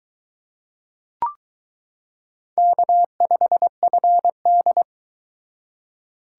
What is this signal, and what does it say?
Morse code, heard as a keyed tone, about 21 wpm, 700 Hz. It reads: K5FD